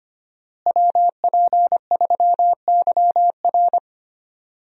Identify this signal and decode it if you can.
Morse code, heard as a keyed tone, about 25 wpm, 700 Hz. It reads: WP3YR
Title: USA Call Signs - 5 Characters - Repeated 2x - 25wpm